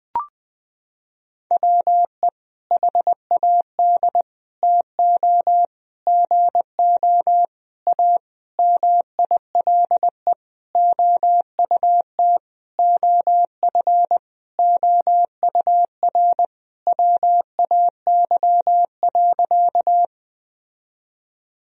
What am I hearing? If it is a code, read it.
Morse code, heard as a keyed tone, about 20 wpm, 700 Hz. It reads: WE HAD TO GO A MILE OUT OF OUR WAY.
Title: Sentences from Top 1000 Words - Review Part 1 - 20wpm